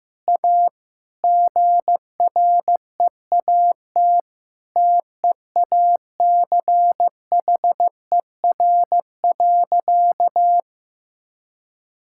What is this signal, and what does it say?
Morse code, heard as a keyed tone, about 16 wpm, 700 Hz. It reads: A GREAT TEACHER.
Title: Sentences from Top 1000 Words - Review Part 6 - 15wpm